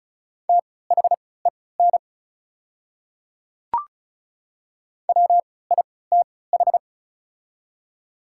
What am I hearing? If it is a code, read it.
Morse code, heard as a keyed tone, about 36 wpm, spaced as for 20 wpm, 690 Hz. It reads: THEN WITH
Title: 3x Character Spacing - Top 100 Words - 35wpm